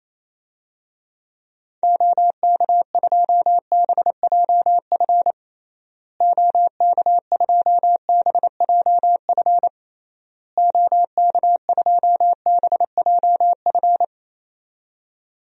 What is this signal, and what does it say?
Morse code, heard as a keyed tone, about 28 wpm, 700 Hz. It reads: OK2BJF OK2BJF OK2BJF
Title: DXCC-100 Call Signs - 6 Characters - Repeated 3x - 28wpm